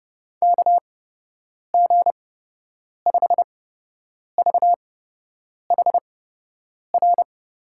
Morse code, heard as a keyed tone, about 30 wpm, spaced as for 8 wpm, 700 Hz. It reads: KG5VHR
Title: USA Call Signs - 6 Characters - 8x Character Spacing - 30wpm